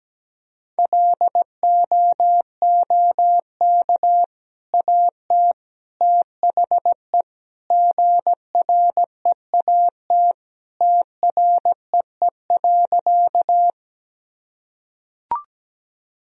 Morse code, heard as a keyed tone, about 17 wpm, 700 Hz. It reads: LOOK AT THE GREAT TREE.
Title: Sentences from Top 1000 Words - Review Part 1 - 17wpm